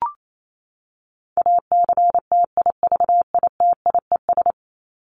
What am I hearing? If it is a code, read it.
Morse code, heard as a keyed tone, about 28 wpm, 700 Hz. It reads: ACTIVITIES
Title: ES:S Suffix - Top 100 Words with Base Words 28wpm